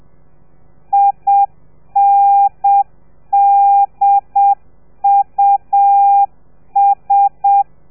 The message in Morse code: INDUS